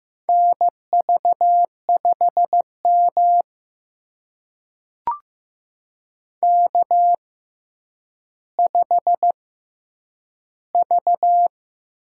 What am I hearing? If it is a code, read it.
Morse code, heard as a keyed tone, about 15 wpm, spaced as for 9 wpm, 700 Hz. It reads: NV5M K5V
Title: USA Call Signs - 4 Characters - 6x Character Spacing - 15wpm